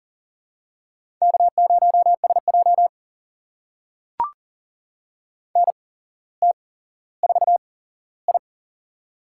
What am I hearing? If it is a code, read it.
Morse code, heard as a keyed tone, about 40 wpm, spaced as for 17 wpm, 700 Hz. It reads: K0SJ NT4I
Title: USA Call Signs - 4 Characters - 8x Character Spacing - 40wpm